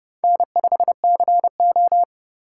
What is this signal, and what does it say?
Morse code, heard as a keyed tone, about 29 wpm, 690 Hz. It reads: N5CO